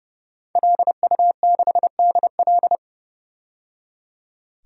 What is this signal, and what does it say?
Morse code, heard as a keyed tone, about 31 wpm, 700 Hz. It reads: LU6DL
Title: DXCC-100 Call Signs - 5 Characters - 30wpm